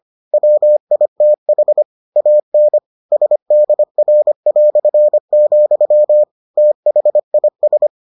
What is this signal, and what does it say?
Morse code, heard as a keyed tone, about 25 wpm, 600 Hz. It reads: WITH AN SDR", THIS